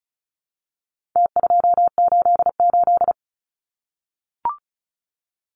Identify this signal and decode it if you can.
Morse code, heard as a keyed tone, about 34 wpm, 700 Hz. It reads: T288
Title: Numbers - Contest Style - 4-Digit 35wpm